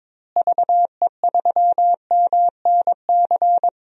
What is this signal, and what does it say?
Morse code, heard as a keyed tone, about 22 wpm, 700 Hz. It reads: VE3MNC
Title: DXCC-100 Call Signs - 6 Characters - Repeated 3x - 22wpm